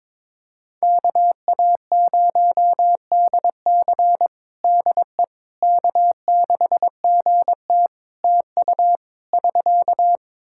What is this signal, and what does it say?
Morse code, heard as a keyed tone, about 22 wpm, 700 Hz. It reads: KA0DC DE K6GT TU <SK>
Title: Practice QSO - Exchange 4 - Ending - 22wpm